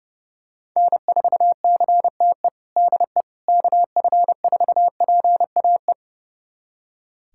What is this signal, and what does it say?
Morse code, heard as a keyed tone, about 30 wpm, 700 Hz. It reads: N4CTE DE KF4PAE
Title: USA Callsign DE Callsign 30wpm